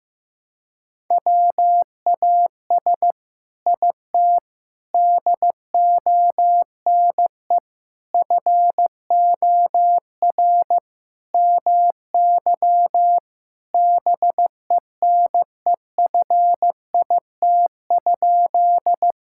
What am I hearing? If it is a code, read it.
Morse code, heard as a keyed tone, about 16 wpm, 700 Hz. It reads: WAS IT DONE FOR MY BENEFIT?
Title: Sentences from Top 1000 Words - Review Part 6 - 15wpm